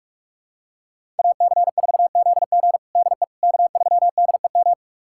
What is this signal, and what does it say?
Morse code, heard as a keyed tone, about 47 wpm, 700 Hz. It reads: AC4ZG DE K3DEM
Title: USA Callsign DE Callsign - 5 Characters - 45wpm